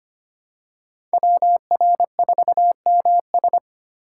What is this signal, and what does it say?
Morse code, heard as a keyed tone, about 25 wpm, 700 Hz. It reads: WR4MS
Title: USA Call Signs - 5 Characters - 3x Character Spacing - 25wpm